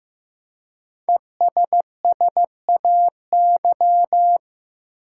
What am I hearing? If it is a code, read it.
Morse code, heard as a keyed tone, about 15 wpm, 700 Hz. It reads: ESSAY
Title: Five Letter Words - 15wpm